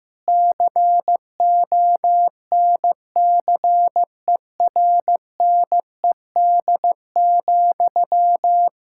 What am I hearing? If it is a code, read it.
Morse code, heard as a keyed tone, about 15 wpm, 700 Hz. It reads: CONCERNED,
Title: Sentences from Top 1000 Words - Review Part 6 - 15wpm